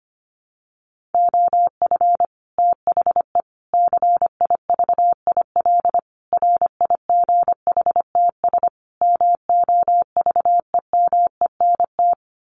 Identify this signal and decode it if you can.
Morse code, heard as a keyed tone, about 25 wpm, 700 Hz. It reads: OF THE CIVIL RIGHTS MOVEMENT
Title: Sets of 5 Words Encore2 25wpm